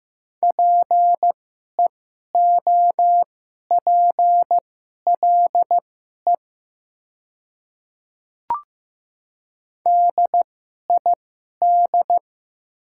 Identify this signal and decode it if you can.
Morse code, heard as a keyed tone, about 15 wpm, spaced as for 11 wpm, 700 Hz. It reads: PEOPLE DID
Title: 2x Character Spacing - Top 100 Words - 15wpm